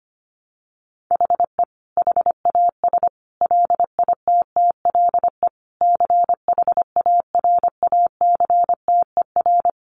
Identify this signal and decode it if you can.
Morse code, heard as a keyed tone, about 25 wpm, 700 Hz. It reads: HE HAS LITTLE CHARACTER
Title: Sentences from Top 1200 Words 25wpm